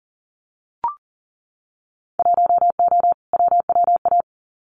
Morse code, heard as a keyed tone, about 42 wpm, 700 Hz. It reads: 1O WWA
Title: Winter Field Day Exchanges 40wpm